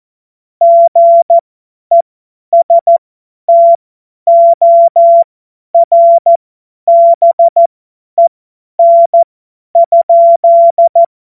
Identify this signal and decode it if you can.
Morse code, heard as a keyed tone, about 14 wpm, spaced as for 10 wpm, 680 Hz. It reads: GESTORBEN?
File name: Morsekurs / Witze / Witz 26. Oktober 2021, 50 F70